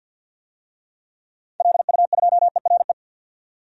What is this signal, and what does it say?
Morse code, heard as a keyed tone, about 52 wpm, 700 Hz. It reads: LU2ERE
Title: DXCC-100 Call Signs - 6 Characters - 50wpm